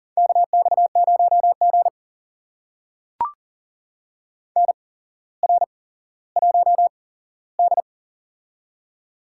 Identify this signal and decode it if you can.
Morse code, heard as a keyed tone, about 40 wpm, spaced as for 17 wpm, 700 Hz. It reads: KX0G NR1D
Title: USA Call Signs - 4 Characters - 8x Character Spacing - 40wpm